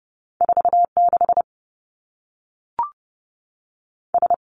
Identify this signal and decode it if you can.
Morse code, heard as a keyed tone, about 30 wpm, 700 Hz. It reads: 4 6 S
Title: Two Challenging Letter-Number Combinations 30wpm